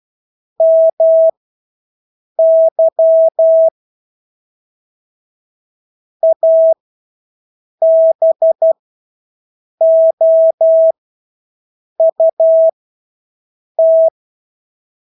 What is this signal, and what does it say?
Morse code, heard as a keyed tone, about 12 wpm, spaced as for 6 wpm, 650 Hz. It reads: MY ABOUT
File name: Engelske ord 12-6 wpm